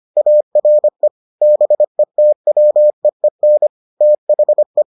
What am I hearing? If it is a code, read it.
Morse code, heard as a keyed tone, about 25 wpm, 600 Hz. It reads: ARE BETWEEN THE